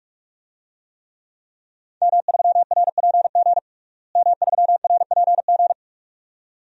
Morse code, heard as a keyed tone, about 46 wpm, 700 Hz. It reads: M3RPG M3RPG
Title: DXCC-100 Call Signs - 5 Characters - Repeated 2x - 45wpm